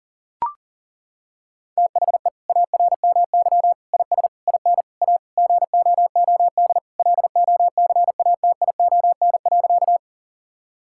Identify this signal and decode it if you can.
Morse code, heard as a keyed tone, about 40 wpm, 700 Hz. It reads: THE ARMY IS IN A GOOD LOCATION.
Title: Sentences from Top 1000 Words - Review Part 6 - 40wpm